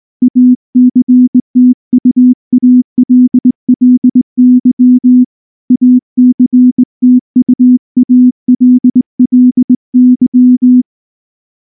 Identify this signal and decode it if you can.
Morse code, heard as a keyed tone, about 19 wpm, 250 Hz. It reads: ACTUALLY ACTUALLY
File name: morse-training-20WPM